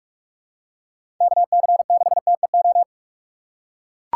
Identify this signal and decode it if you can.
Morse code, heard as a keyed tone, about 41 wpm, 700 Hz. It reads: KC6TEO
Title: USA Call Signs - 6 Characters - 2x Character Spacing - 45wpm